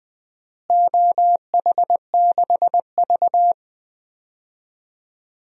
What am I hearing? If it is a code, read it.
Morse code, heard as a keyed tone, about 20 wpm, 700 Hz. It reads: OH6V
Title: DXCC-100 Call Signs - 4 Characters - Repeated 2x - 20wpm